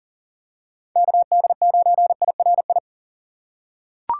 Morse code, heard as a keyed tone, about 40 wpm, 700 Hz. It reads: KD9IRI
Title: USA Call Signs - 6 Characters - 2x Character Spacing - 40wpm